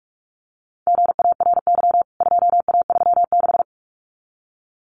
Morse code, heard as a keyed tone, about 44 wpm, 700 Hz. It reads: GARY 2A36
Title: CWT Exchange 45wpm